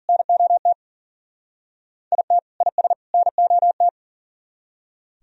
Morse code, heard as a keyed tone, about 51 wpm, 700 Hz. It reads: NOT IT IS NOT